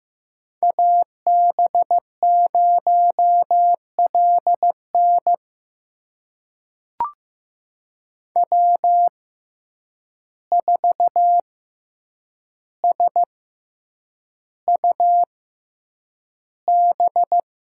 Morse code, heard as a keyed tone, about 15 wpm, spaced as for 8 wpm, 700 Hz. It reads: AB0LN W4SUB
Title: USA Call Signs - 5 Characters - 6x Character Spacing - 15wpm